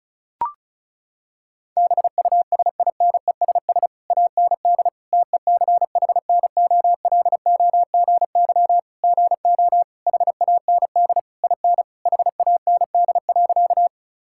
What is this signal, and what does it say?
Morse code, heard as a keyed tone, about 36 wpm, 700 Hz. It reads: BUSINESS AND TECHNOLOGY GO HAND IN HAND.
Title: Sentences from Top 1000 Words - Review Part 5 - 35wpm